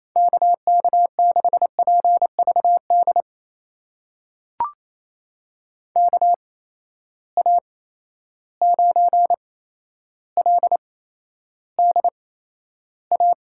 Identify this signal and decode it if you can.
Morse code, heard as a keyed tone, about 28 wpm, spaced as for 12 wpm, 700 Hz. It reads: KK6PVD KA9LDA
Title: USA Call Signs - 6 Characters - 8x Character Spacing - 28wpm